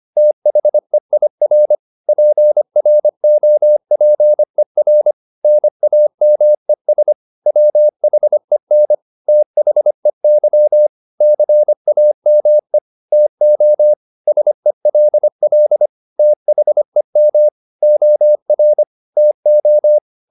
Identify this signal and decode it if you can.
Morse code, heard as a keyed tone, about 25 wpm, 600 Hz. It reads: THEIR PROPER NAMES WHEN THEY CAME TO SELL THEM OR TO
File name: the old man and the sea part 1 25wpm